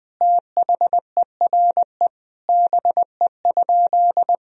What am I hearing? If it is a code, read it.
Morse code, heard as a keyed tone, about 20 wpm, 700 Hz. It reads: THERE BE?
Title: Sentences from Top 1000 Words - Review Part 3 - 20wpm